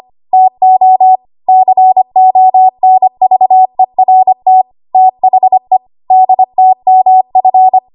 Morse code, heard as a keyed tone, about 25 wpm, 750 Hz. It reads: TO CONVERT THE DTMF
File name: Morse / 25WPM